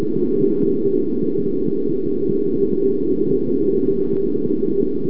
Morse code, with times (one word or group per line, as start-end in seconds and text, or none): none